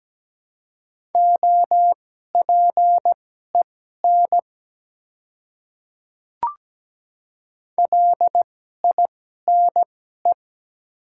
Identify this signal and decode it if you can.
Morse code, heard as a keyed tone, about 17 wpm, spaced as for 12 wpm, 690 Hz. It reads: OPEN LINE
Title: 2x Character Spacing - Top 300 Words - 17wpm